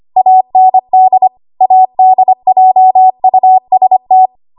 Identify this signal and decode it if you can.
Morse code, heard as a keyed tone, about 25 wpm, 750 Hz. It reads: AND ADJUST
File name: Archive / 25 WPM / 25WPM